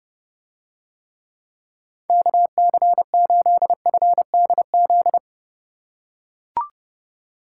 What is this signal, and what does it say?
Morse code, heard as a keyed tone, about 29 wpm, 700 Hz. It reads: KC8FDZ